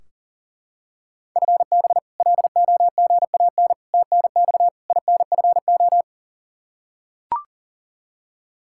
Morse code, heard as a keyed tone, about 40 wpm, 700 Hz. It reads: FB LOGAN TNX INFO
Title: K1USN SST QSO Practice 40wpm